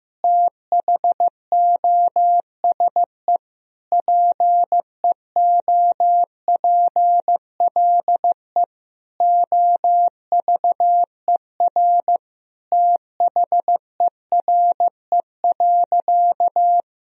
Morse code, heard as a keyed tone, about 16 wpm, 700 Hz. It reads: THOSE PEOPLE OVER THERE.
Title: Sentences from Top 1000 Words - Review Part 1 - 15wpm